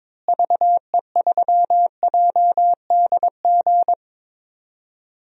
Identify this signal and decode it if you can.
Morse code, heard as a keyed tone, about 22 wpm, 700 Hz. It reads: VE3JDG
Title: DXCC-100 Call Signs - 6 Characters - Repeated 3x - 22wpm